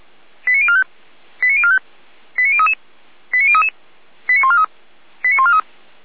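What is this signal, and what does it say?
Five-tone selective call (ZVEI), heard as quick runs of five tones, tones of about 70 ms. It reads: 89045 89045 89033 89033 89143 89143
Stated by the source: Five-tone ZVEI selective call received on an SDR.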